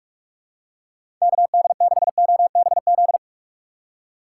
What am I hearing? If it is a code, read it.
Morse code, heard as a keyed tone, about 44 wpm, 700 Hz. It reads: KD6OBZ
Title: USA Call Signs - 6 Characters - 6x Character Spacing - 45wpm